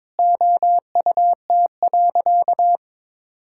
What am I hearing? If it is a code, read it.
Morse code, heard as a keyed tone, about 22 wpm, 700 Hz. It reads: OUT.
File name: practice / 22wpm / Sentences with 3-Word Phrases - 22wpm